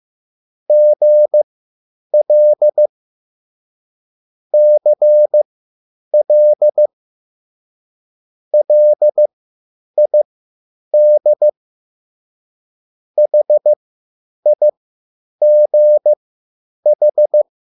Morse code, heard as a keyed tone, about 15 wpm, spaced as for 9 wpm, 600 Hz. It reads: GL CL LID HIGH
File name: QSL / CW lesson